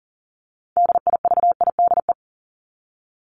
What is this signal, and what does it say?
Morse code, heard as a keyed tone, about 40 wpm, 700 Hz. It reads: DIVIDE
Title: ED Suffix - Top 100 Words with Base Words 40wpm